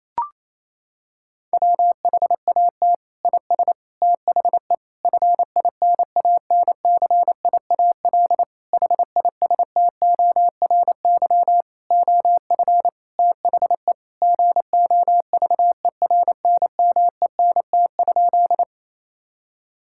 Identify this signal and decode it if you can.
Morse code, heard as a keyed tone, about 28 wpm, 700 Hz. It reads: WHAT IS THE FINANCIAL HISTORY OF THE GOVERNMENT?
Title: Sentences from Top 1000 Words - Review Part 5 - 28wpm